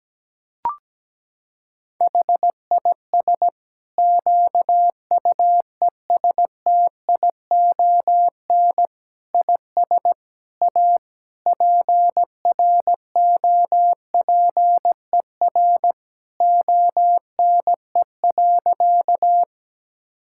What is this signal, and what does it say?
Morse code, heard as a keyed tone, about 17 wpm, 700 Hz. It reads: HIS QUESTION IS A PROPER ONE.
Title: Sentences from Top 1000 Words - Review Part 5 - 17wpm